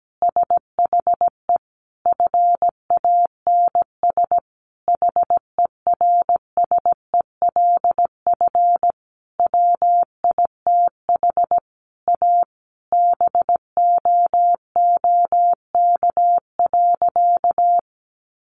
Morse code, heard as a keyed tone, about 17 wpm, 700 Hz. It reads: SHE FANS HERSELF WITH A BOOK.
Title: Easy Phrases 17wpm